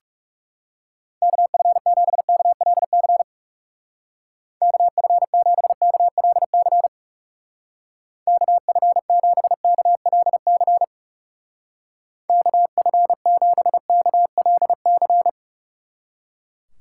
Morse code, heard as a keyed tone, about 37 wpm, 700 Hz. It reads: KF7KLC KF7KLC KF7KLC KF7KLC
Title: Speed Racing - USA 2x3 Call Signs 30wpm